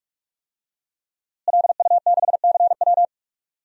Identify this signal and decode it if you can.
Morse code, heard as a keyed tone, about 44 wpm, 700 Hz. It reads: LU6CW
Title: DXCC-100 Call Signs - 5 Characters - Repeated 2x - 45wpm